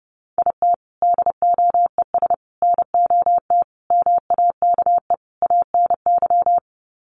Morse code, heard as a keyed tone, about 30 wpm, 700 Hz. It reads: IT DOES NOT MAKE ANY
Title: Sets of 5 Words Encore2 30wpm